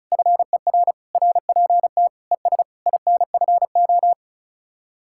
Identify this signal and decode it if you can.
Morse code, heard as a keyed tone, about 43 wpm, 700 Hz. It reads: FER RPT ES INFO
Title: Speed-Racing Common QSO Phrases - 35wpm